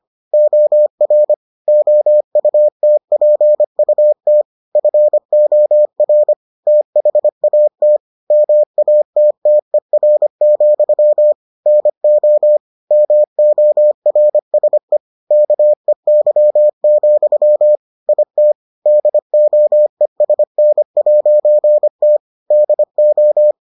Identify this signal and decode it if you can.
Morse code, heard as a keyed tone, about 25 wpm, 600 Hz. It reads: OR OUTPUT FOR THAT MATTER, NO MORSE KEY, IT DOESN'T DO